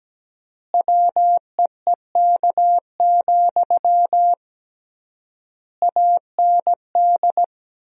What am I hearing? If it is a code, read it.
Morse code, heard as a keyed tone, about 17 wpm, 700 Hz. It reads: WEEK, AND
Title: 3x Word-Spacing - Sentences from Top 500 Words - 17wpm